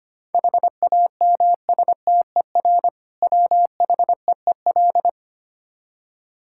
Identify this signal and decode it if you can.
Morse code, heard as a keyed tone, about 25 wpm, 700 Hz. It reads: HAMSTER WHEEL
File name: practice / 25wpm / Difficult Words with Hints - 25wpm